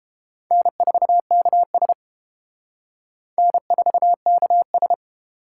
Morse code, heard as a keyed tone, about 28 wpm, 700 Hz. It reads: N4KS N4KS